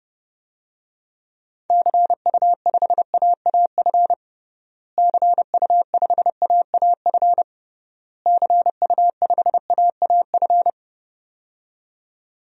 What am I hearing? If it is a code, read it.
Morse code, heard as a keyed tone, about 30 wpm, 700 Hz. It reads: CU5AAF CU5AAF CU5AAF
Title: DXCC-100 Call Signs - 6 Characters - Repeated 3x - 30wpm